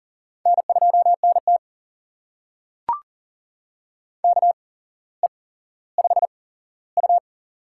Morse code, heard as a keyed tone, about 40 wpm, spaced as for 17 wpm, 700 Hz. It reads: N2NT KE5U
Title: USA Call Signs - 4 Characters - 8x Character Spacing - 40wpm